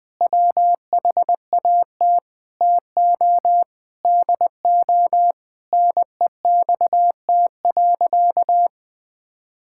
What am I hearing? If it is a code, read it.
Morse code, heard as a keyed tone, about 20 wpm, 700 Hz. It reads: WHAT TO DO NEXT.
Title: Sentences with 6-Word Phrases - 20wpm